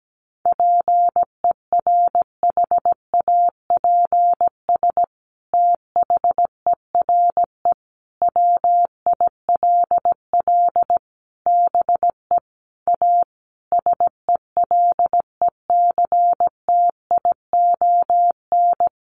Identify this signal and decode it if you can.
Morse code, heard as a keyed tone, about 17 wpm, 700 Hz. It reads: PERHAPS THERE WILL BE A SELECTION
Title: Sentences from Top 1400 Words 17wpm